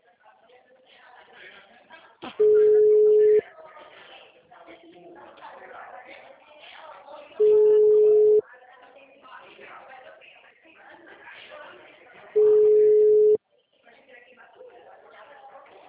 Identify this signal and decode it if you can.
Ringback tone (Europe, 425 Hz), with speech.
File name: out- -2009- - - .73252